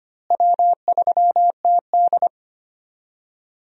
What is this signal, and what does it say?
Morse code, heard as a keyed tone, about 25 wpm, 700 Hz. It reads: W3TD
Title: USA Call Signs - 4 Characters - Repeated 3x - 25wpm